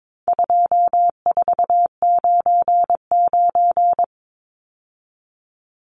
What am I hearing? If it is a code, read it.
Morse code, heard as a keyed tone, about 22 wpm, 700 Hz. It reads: 2499
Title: Numbers - 4-Digit 22wpm